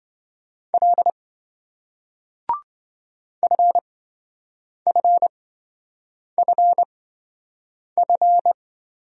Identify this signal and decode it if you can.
Morse code, heard as a keyed tone, about 27 wpm, spaced as for 7 wpm, 700 Hz. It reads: L FFFF